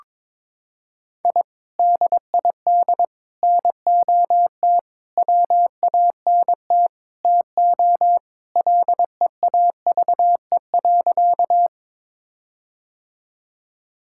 Morse code, heard as a keyed tone, about 22 wpm, 700 Hz. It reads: I DID NOT WANT TO LEAVE.
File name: practice / 22wpm / Sentences with 4-Word Phrases - 22wpm